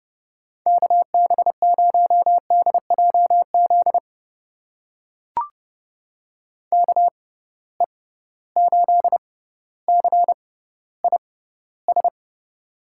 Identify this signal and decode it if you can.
Morse code, heard as a keyed tone, about 30 wpm, spaced as for 15 wpm, 700 Hz. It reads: KB0DJZ KE8CIS
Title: USA Call Signs - 6 Characters - 6x Character Spacing - 30wpm